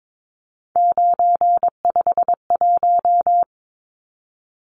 Morse code, heard as a keyed tone, about 22 wpm, 700 Hz. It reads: 951
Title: Numbers - 3-Digit 22wpm